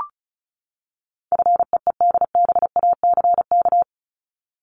Morse code, heard as a keyed tone, about 34 wpm, 700 Hz. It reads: FEEDBACK
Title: Top 1501-1750 Words 35wpm